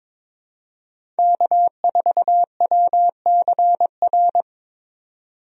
Morse code, heard as a keyed tone, about 22 wpm, 700 Hz. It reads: K4WCR
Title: USA Call Signs - 5 Characters - 22wpm